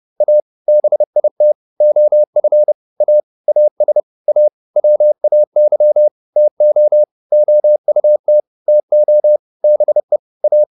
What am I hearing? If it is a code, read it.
Morse code, heard as a keyed tone, about 30 wpm, 600 Hz. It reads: A BIT OF A AS A WAY TO OUT TO BE A